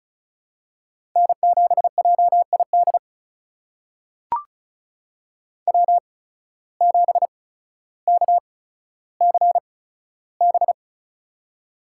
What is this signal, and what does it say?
Morse code, heard as a keyed tone, about 36 wpm, spaced as for 15 wpm, 700 Hz. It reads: N7JID W7KCB
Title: USA Call Signs - 5 Characters - 8x Character Spacing - 35wpm